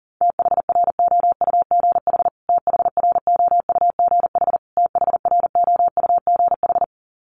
Morse code, heard as a keyed tone, about 40 wpm, 700 Hz. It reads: THROUGH THROUGH THROUGH
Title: Top 200 Words Repeated 40wpm